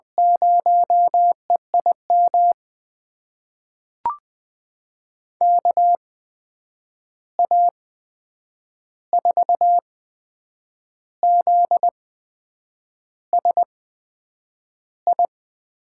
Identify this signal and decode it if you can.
Morse code, heard as a keyed tone, about 20 wpm, spaced as for 8 wpm, 700 Hz. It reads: 0EIM KA4ZSI
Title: USA Call Signs - 6 Characters - 8x Character Spacing - 20wpm